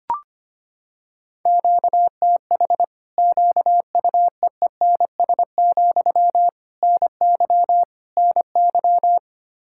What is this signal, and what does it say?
Morse code, heard as a keyed tone, about 25 wpm, 700 Hz. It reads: QTH QUEENS, NY NY
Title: Practice QSO - Exchange 1 - Beginning - 25wpm